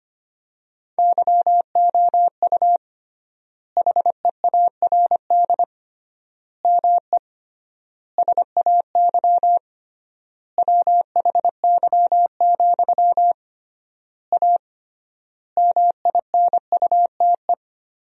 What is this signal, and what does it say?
Morse code, heard as a keyed tone, about 25 wpm, 700 Hz. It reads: YOU HEARD ME SAY WHY, A MINUTE
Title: 3x Word-Spacing - Sentences from Top 500 Words - 25wpm